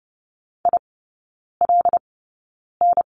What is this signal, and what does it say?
Morse code, heard as a keyed tone, about 30 wpm, spaced as for 9 wpm, 700 Hz. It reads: ILN
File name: Lesson 15e - Number 5 - Three Characters 30wpm